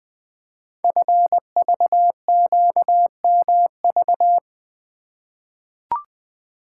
Morse code, heard as a keyed tone, about 20 wpm, 700 Hz. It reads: FVQMV
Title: Code Groups - 5 Letter:Numbers - 20wpm